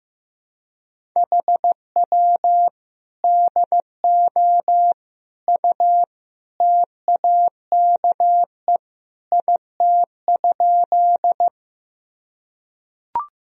Morse code, heard as a keyed tone, about 15 wpm, 700 Hz. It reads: HW DO U TAKE IT?
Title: Sentences from Top 200 Words and Abbreviations 15wpm